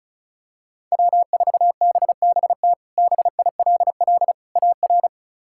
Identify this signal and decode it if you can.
Morse code, heard as a keyed tone, about 36 wpm, 700 Hz. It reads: W4BBT BILL AR
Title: K1USN SST QSO Practice - 35wpm